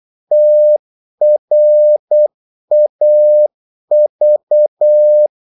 Morse code, heard as a keyed tone, about 8 wpm, 600 Hz. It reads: TRAV
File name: Korte ord DK 8wpm